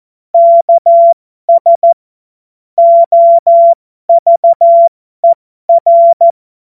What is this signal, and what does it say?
Morse code, heard as a keyed tone, about 14 wpm, spaced as for 12 wpm, 680 Hz. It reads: KS OVER